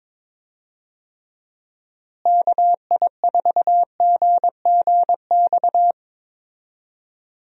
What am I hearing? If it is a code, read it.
Morse code, heard as a keyed tone, about 22 wpm, 700 Hz. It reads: KI4GGX
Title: USA Call Signs - 6 Characters - Repeated 3x - 22wpm